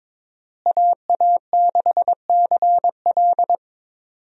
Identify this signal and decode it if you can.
Morse code, heard as a keyed tone, about 22 wpm, 700 Hz. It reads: AA6CL
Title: USA Call Signs - 5 Characters - Repeated 2x - 22wpm